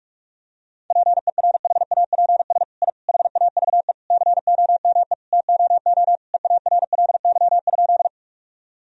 Morse code, heard as a keyed tone, about 41 wpm, 700 Hz. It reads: PERHAPS I HAVE COME TOO EARLY?
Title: Sentences from Top 1000 Words - Review Part 2 - 45wpm